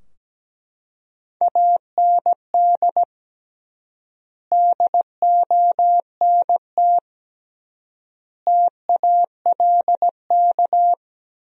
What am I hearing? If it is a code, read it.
Morse code, heard as a keyed tone, about 17 wpm, 700 Hz. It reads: AND DONT TALK
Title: 3x Word-Spacing - Sentences from Top 400 Words - 17wpm